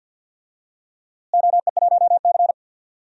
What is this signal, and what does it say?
Morse code, heard as a keyed tone, about 53 wpm, 700 Hz. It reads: OE1C
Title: DXCC-100 Call Signs - 4 Characters - Repeated 3x - 50wpm